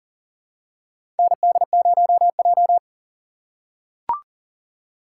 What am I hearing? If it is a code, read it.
Morse code, heard as a keyed tone, about 40 wpm, 700 Hz. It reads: ND0J
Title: USA Call Signs - 4 Characters - 4x Character Spacing - 40wpm